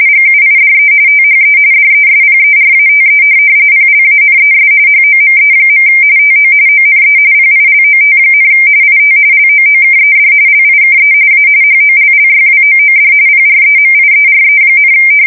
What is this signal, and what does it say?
RTTY, about 45.45 baud. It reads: GUIDE THE DESPERATE PLANES HOME. TO ONE PILOT, IT LOOKED LIKE A HOLLYWOOD PREMIERE,
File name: fsk170 / USS-New-Jersey